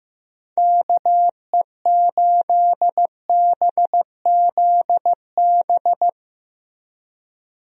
Morse code, heard as a keyed tone, about 15 wpm, 700 Hz. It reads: KE8BZB
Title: USA Call Signs - 6 Characters - Repeated 3x - 15wpm